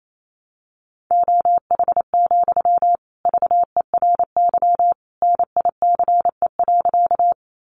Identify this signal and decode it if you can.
Morse code, heard as a keyed tone, about 28 wpm, 700 Hz. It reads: OH, VERY NICE.
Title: Sentences from Top 1100 Words 28wpm